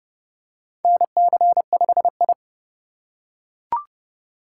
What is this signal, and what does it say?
Morse code, heard as a keyed tone, about 29 wpm, 700 Hz. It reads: NC5I
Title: USA Call Signs - 4 Characters - 3x Character Spacing - 30wpm